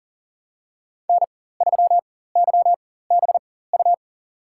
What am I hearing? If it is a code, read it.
Morse code, heard as a keyed tone, about 40 wpm, spaced as for 19 wpm, 700 Hz. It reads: N3YBU
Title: USA Call Signs - 5 Characters - 4x Character Spacing - 40wpm